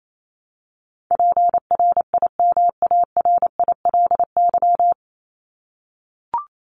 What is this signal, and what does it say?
Morse code, heard as a keyed tone, about 28 wpm, 700 Hz. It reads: PRIMARILY
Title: LY Suffix - Top 100 Words with Base Words 28wpm